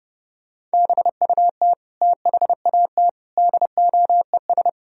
Morse code, heard as a keyed tone, about 29 wpm, 700 Hz. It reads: BUT THAT DOES